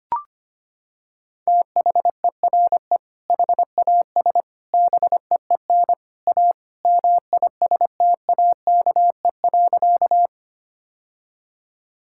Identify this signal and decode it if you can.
Morse code, heard as a keyed tone, about 25 wpm, 700 Hz. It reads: THERE HAS BEEN A MISTAKE.
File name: practice / 25wpm / Sentences with 3-Word Phrases - 25wpm